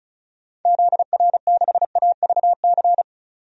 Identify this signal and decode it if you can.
Morse code, heard as a keyed tone, about 36 wpm, 690 Hz. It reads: ZR6AVC
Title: DXCC-100 Call Signs - 6 Characters - 35wpm